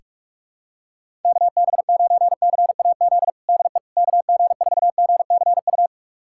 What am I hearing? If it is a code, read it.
Morse code, heard as a keyed tone, about 46 wpm, 700 Hz. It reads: KB9CAZ DE KG4GCU
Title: USA Callsign DE Callsign 45wpm